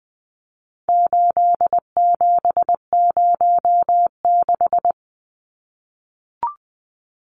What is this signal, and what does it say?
Morse code, heard as a keyed tone, about 19 wpm, 700 Hz. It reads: 8706
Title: Numbers - 4-Digit 20wpm